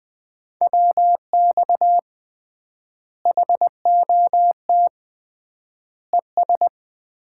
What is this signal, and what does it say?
Morse code, heard as a keyed tone, about 20 wpm, 700 Hz. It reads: WX HOT ES
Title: Common QSO Phrases - 3x Word-Spacing - 20wpm